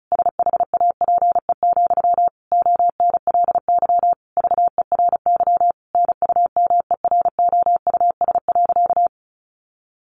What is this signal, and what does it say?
Morse code, heard as a keyed tone, about 34 wpm, 700 Hz. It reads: SHAPE, ONLY VERY NUMEROUS.
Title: Sentences from Top 1400 Words 35wpm